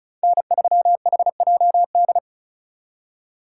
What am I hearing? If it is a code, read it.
Morse code, heard as a keyed tone, about 39 wpm, 700 Hz. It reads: N3HJD
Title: Speed Racing - USA 1x3 Call Signs 35wpm